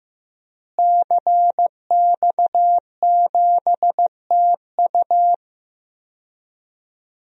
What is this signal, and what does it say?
Morse code, heard as a keyed tone, about 15 wpm, 700 Hz. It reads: CX7TU
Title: DXCC-100 Call Signs - 5 Characters - Repeated 3x - 15wpm